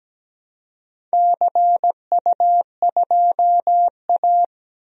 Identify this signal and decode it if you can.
Morse code, heard as a keyed tone, about 17 wpm, 700 Hz. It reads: CU2A
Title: DXCC-100 Call Signs - 4 Characters - 17wpm